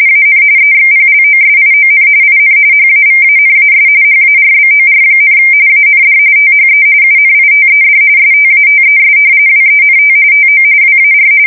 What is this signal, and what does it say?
RTTY, about 45.45 baud. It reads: AN EXPLOSION IN HAVANAS HARBOR, AND THE UNITED STATES WENT TO WAR
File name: fsk170 / battleships